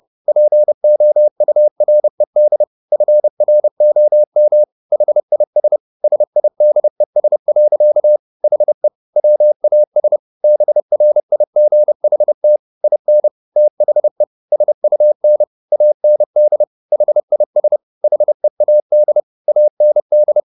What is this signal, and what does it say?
Morse code, heard as a keyed tone, about 30 wpm, 600 Hz. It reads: POURED FROM HIS SIDES. HE WAS BRIGHT IN THE SUN AND HIS HEAD AND
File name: the old man and the sea / the old man and the sea 1 30wpm